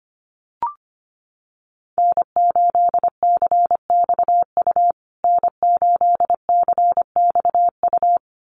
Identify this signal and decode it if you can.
Morse code, heard as a keyed tone, about 24 wpm, 700 Hz. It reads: N8CXU N8CXU
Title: USA Call Signs Repeated 2x 25wpm